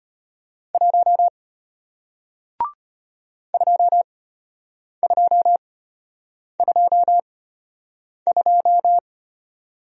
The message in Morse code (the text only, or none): 1 2222